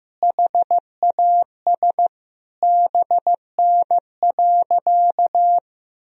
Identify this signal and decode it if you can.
Morse code, heard as a keyed tone, about 15 wpm, 700 Hz. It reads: HAS BN.
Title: Sentences from Top 300 Words and Abbreviations 15wpm